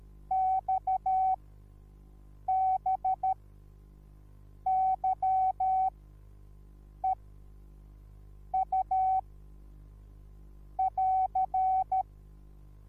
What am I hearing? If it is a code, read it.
Morse code, heard as a keyed tone, about 13 wpm, spaced as for 6 wpm, 760 Hz. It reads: XBYEU <AR>
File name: Onlinelehrgang / morsen / DJ4UF Morsen21